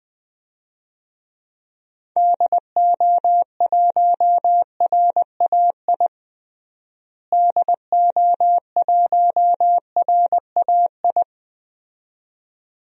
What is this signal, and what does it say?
Morse code, heard as a keyed tone, about 21 wpm, 700 Hz. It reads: DO1RAI DO1RAI
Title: DXCC-100 Call Signs - 6 Characters - Repeated 2x - 20wpm